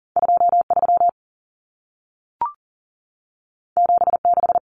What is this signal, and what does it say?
Morse code, heard as a keyed tone, about 40 wpm, 700 Hz. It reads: 23 76
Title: Numbers - 2-Digit 40wpm